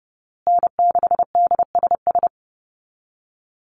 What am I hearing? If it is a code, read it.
Morse code, heard as a keyed tone, about 30 wpm, 700 Hz. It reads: N6DSS
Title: USA Call Signs Repeated 2x 30wpm